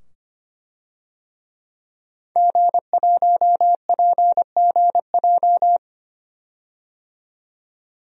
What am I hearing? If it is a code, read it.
Morse code, heard as a keyed tone, about 25 wpm, 700 Hz. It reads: G1PGJ
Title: DXCC-100 Call Signs - 5 Characters - 25wpm